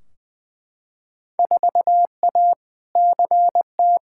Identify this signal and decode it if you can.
Morse code, heard as a keyed tone, about 20 wpm, 700 Hz. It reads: 4A CT
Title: Field Day Exchanges Encore - 20wpm